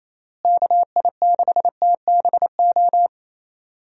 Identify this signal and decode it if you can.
Morse code, heard as a keyed tone, about 27 wpm, 700 Hz. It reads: KI6TBO
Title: USA Call Signs - 6 Characters - Repeated 3x - 28wpm